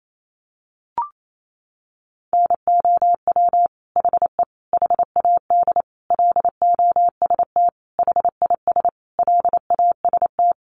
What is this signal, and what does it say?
Morse code, heard as a keyed tone, about 28 wpm, 700 Hz. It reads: NOW HE HAD LOST HIS LAST
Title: Sentences from Top 1400 Words 28wpm